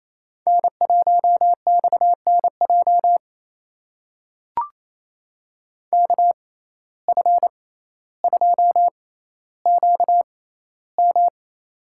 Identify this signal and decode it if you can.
Morse code, heard as a keyed tone, about 28 wpm, spaced as for 14 wpm, 700 Hz. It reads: N1XNJ KF2QM
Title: USA Call Signs - 5 Characters - 6x Character Spacing - 28wpm